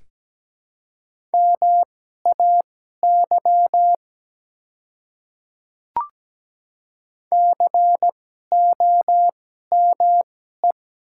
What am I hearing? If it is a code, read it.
Morse code, heard as a keyed tone, about 17 wpm, spaced as for 12 wpm, 700 Hz. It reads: MAY COME
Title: 2x Character Spacing - Top 200 Words - 17wpm